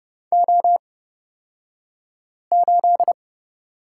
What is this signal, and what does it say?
Morse code, heard as a keyed tone, about 30 wpm, spaced as for 5 wpm, 700 Hz. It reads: O8
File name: Lesson 36d - BT - Two Characters 30wpm